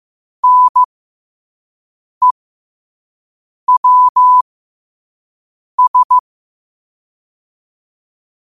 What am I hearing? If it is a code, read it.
Morse code, heard as a keyed tone, about 15 wpm, spaced as for 5 wpm, 1000 Hz. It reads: NEWS
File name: qotdfiles / 5wpm